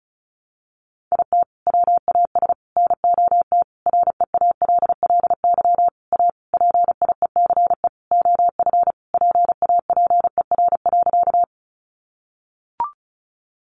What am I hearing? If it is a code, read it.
Morse code, heard as a keyed tone, about 34 wpm, 700 Hz. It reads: IT WAS NOT REALLY A PIECE OF PAPER.
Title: Sentences from Top 600 Words 35wpm